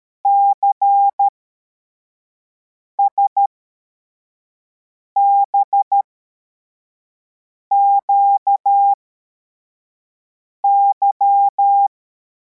Morse code, heard as a keyed tone, about 13 wpm, spaced as for 4 wpm, 800 Hz. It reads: CSBQY
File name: formation / cw / Text74